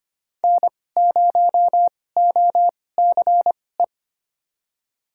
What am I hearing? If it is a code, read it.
Morse code, heard as a keyed tone, about 25 wpm, spaced as for 18 wpm, 700 Hz. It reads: N0OCE
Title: USA Call Signs - 5 Characters - 2x Character Spacing - 25wpm